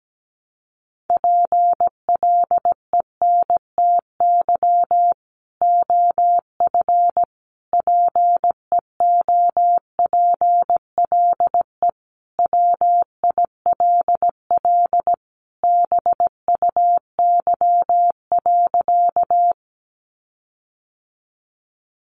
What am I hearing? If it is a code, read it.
Morse code, heard as a keyed tone, about 17 wpm, 700 Hz. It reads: PLENTY OF PEOPLE WILL BUY.
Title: Sentences from Top 1300 Words 17wpm